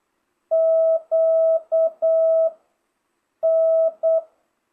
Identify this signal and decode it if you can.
Morse code, heard as a keyed tone, about 8 wpm, 650 Hz. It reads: Q N